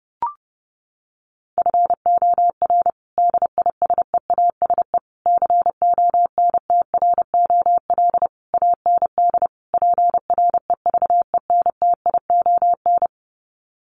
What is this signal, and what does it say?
Morse code, heard as a keyed tone, about 30 wpm, 700 Hz. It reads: FOR DISEASE CONTROL AND PREVENTION
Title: Sets of 5 Words Encore2 30wpm